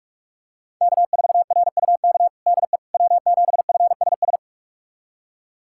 Morse code, heard as a keyed tone, about 46 wpm, 700 Hz. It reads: K4RUK DE W7FSS
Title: USA Callsign DE Callsign - 5 Characters - 45wpm